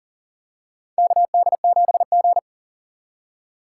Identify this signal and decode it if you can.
Morse code, heard as a keyed tone, about 40 wpm, 700 Hz. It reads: KD7G
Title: USA Call Signs - 4 Characters - 2x Character Spacing - 40wpm